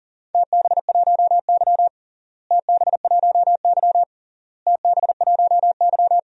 Morse code, heard as a keyed tone, about 37 wpm, 700 Hz. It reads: TB1Y TB1Y TB1Y
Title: DXCC-100 Call Signs - 4 Characters - Repeated 3x - 40wpm